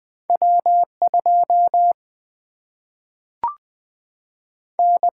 Morse code, heard as a keyed tone, about 20 wpm, 700 Hz. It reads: W 2 N
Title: Single-Letter Single-Number 20wpm